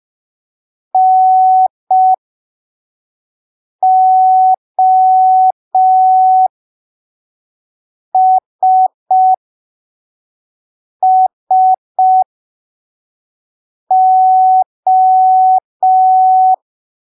Morse code, heard as a keyed tone, about 5 wpm, spaced as for 3 wpm, 750 Hz. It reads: NOSSO
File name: arquivos / texto / texto 22 05ppm